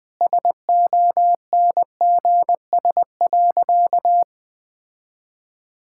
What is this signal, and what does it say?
Morse code, heard as a keyed tone, about 21 wpm, 700 Hz. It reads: SONGS.
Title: Sentences with 4-Word Phrases - 20wpm